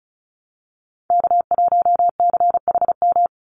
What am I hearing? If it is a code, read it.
Morse code, heard as a keyed tone, about 34 wpm, 700 Hz. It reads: K1CHM
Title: Call Signs Repeated 3x - 35wpm